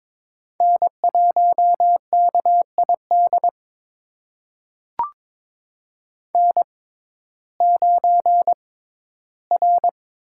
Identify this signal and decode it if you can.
Morse code, heard as a keyed tone, about 22 wpm, spaced as for 14 wpm, 700 Hz. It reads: N1KID N9R
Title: USA Call Signs - 6x Character Spacing - 22wpm